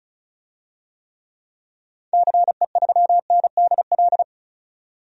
Morse code, heard as a keyed tone, about 36 wpm, 700 Hz. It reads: CE3NDL
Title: DXCC-100 Call Signs - 6 Characters - 35wpm